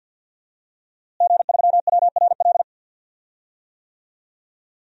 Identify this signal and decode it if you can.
Morse code, heard as a keyed tone, about 52 wpm, 700 Hz. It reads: G3WRL
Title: DXCC-100 Call Signs - 5 Characters - Repeated 2x - 50wpm